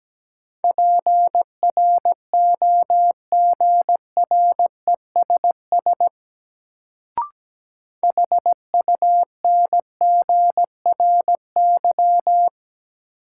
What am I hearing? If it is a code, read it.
Morse code, heard as a keyed tone, about 17 wpm, 700 Hz. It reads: PROGRESS HUNGRY
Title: Sentences from Top 1400 Words - New Word Drill Alt - 17wpm